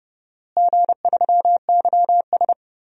Morse code, heard as a keyed tone, about 30 wpm, 700 Hz. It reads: G3YS
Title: DXCC-100 Call Signs - 4 Characters - Repeated 2x - 30wpm